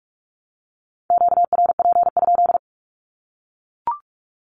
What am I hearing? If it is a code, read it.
Morse code, heard as a keyed tone, about 44 wpm, 700 Hz. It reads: QRP?
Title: Q-Codes 45wpm